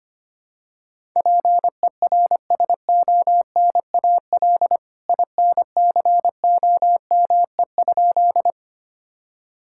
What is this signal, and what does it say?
Morse code, heard as a keyed tone, about 25 wpm, 700 Hz. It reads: PERSONAL INCOME?
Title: Sentences from Top 1000 Words - Review Part 5 - 25wpm